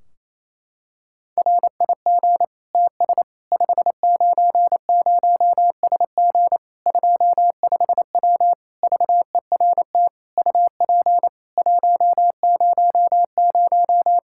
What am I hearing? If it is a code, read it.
Morse code, heard as a keyed tone, about 28 wpm, 700 Hz. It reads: RIG TS 590SG 25W VERT UP 100
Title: Practice QSO - Exchange 2 - Info - 28wpm